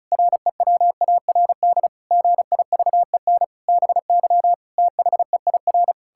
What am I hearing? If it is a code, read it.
Morse code, heard as a keyed tone, about 36 wpm, 700 Hz. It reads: REWARD GIVEN BY THEIR
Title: Pangram Challenge v2 - 35wpm